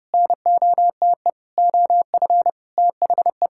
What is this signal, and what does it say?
Morse code, heard as a keyed tone, about 32 wpm, 700 Hz. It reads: NOTE OF THE
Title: Sentences from Top 1600 Words 30wpm